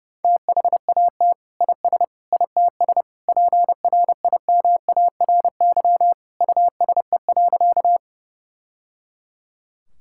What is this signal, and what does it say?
Morse code, heard as a keyed tone, about 31 wpm, 700 Hz. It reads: THAT IS ITS PRIMARY USE.
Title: Sentences from Top 1000 Words - Review Part 5 - 30wpm